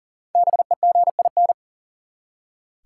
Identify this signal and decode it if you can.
Morse code, heard as a keyed tone, about 40 wpm, 700 Hz. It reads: BEGIN
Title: Five Letter Words - 40wpm